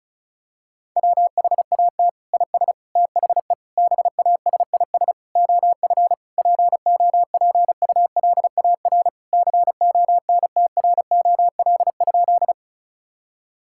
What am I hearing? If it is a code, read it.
Morse code, heard as a keyed tone, about 36 wpm, 700 Hz. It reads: WHAT IS THE BASIS OF POPULAR CONTROL?
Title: Sentences from Top 1000 Words - Review Part 5 - 35wpm